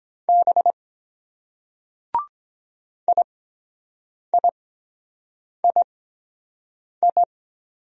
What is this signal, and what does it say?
Morse code, heard as a keyed tone, about 23 wpm, spaced as for 7 wpm, 700 Hz. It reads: B IIII